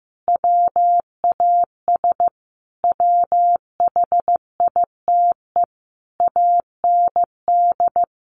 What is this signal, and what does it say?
Morse code, heard as a keyed tone, about 14 wpm, 700 Hz. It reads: WAS WHITE AND
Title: Sentences from Top 1200 Words 15wpm